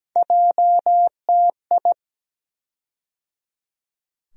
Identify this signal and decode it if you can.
Morse code, heard as a keyed tone, about 17 wpm, 700 Hz. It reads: JTI
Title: USA Call Sign Postfixes - 3 Letters 17wpm